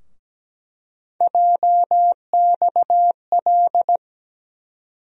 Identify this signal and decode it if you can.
Morse code, heard as a keyed tone, about 17 wpm, 700 Hz. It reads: JXL
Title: USA Call Sign Postfixes - 3 Letters 17wpm